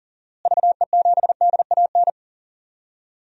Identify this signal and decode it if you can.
Morse code, heard as a keyed tone, about 40 wpm, 700 Hz. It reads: VE7DAN
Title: DXCC-100 Call Signs - 6 Characters - Repeated 2x - 40wpm